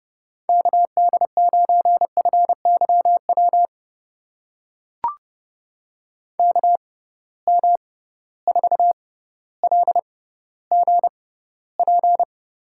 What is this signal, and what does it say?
Morse code, heard as a keyed tone, about 30 wpm, spaced as for 15 wpm, 700 Hz. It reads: KD9FYW KM4LGP
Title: USA Call Signs - 6 Characters - 6x Character Spacing - 30wpm